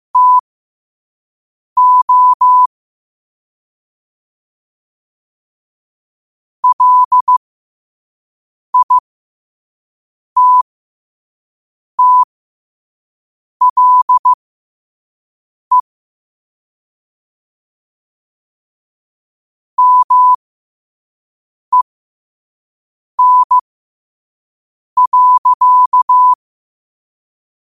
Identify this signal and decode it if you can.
Morse code, heard as a keyed tone, about 15 wpm, spaced as for 5 wpm, 1000 Hz. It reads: TO LITTLE MEN.
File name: qotdfiles / 5wpm